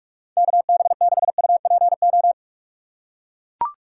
Morse code, heard as a keyed tone, about 42 wpm, 700 Hz. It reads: KB6UPO
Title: USA Call Signs - 6 Characters - 6x Character Spacing - 45wpm